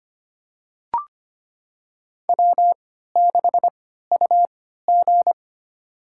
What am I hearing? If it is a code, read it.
Morse code, heard as a keyed tone, about 25 wpm, spaced as for 14 wpm, 700 Hz. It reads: W6UG
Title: USA Call Signs - 4 Characters - 3x Character Spacing - 25wpm